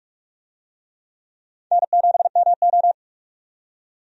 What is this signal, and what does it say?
Morse code, heard as a keyed tone, about 46 wpm, 700 Hz. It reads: N7MO
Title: USA Call Signs - 4 Characters - 45wpm